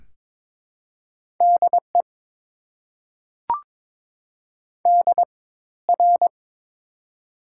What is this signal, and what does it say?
Morse code, heard as a keyed tone, about 22 wpm, spaced as for 14 wpm, 700 Hz. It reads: DE DR
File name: Top 90 QSO Elements - 4x Character Spacing - 22wpm